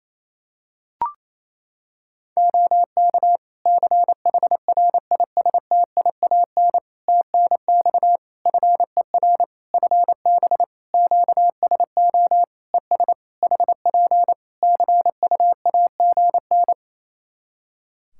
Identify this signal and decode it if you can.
Morse code, heard as a keyed tone, about 28 wpm, 700 Hz. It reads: OK CHRISTIAN TNX FER FB QSO ES HP CUAGN
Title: Practice QSO - Exchange 4 - Ending - 28wpm